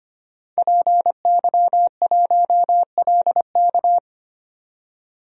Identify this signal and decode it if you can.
Morse code, heard as a keyed tone, about 25 wpm, 700 Hz. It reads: PY1LK
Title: DXCC-100 Call Signs - 5 Characters - 25wpm